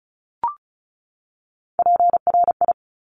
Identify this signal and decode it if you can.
Morse code, heard as a keyed tone, about 34 wpm, 700 Hz. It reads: PRI
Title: Top Three Letter Combinations in Words 35wpm